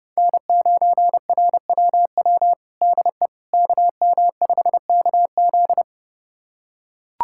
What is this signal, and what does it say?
Morse code, heard as a keyed tone, about 32 wpm, 700 Hz. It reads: N9RWW DE KM5KZ
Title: USA Callsign DE Callsign - 5 Characters - 30wpm